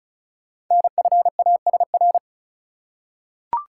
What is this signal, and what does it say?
Morse code, heard as a keyed tone, about 36 wpm, 700 Hz. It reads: NFASR
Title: Code Groups - 5 Letters - 35wpm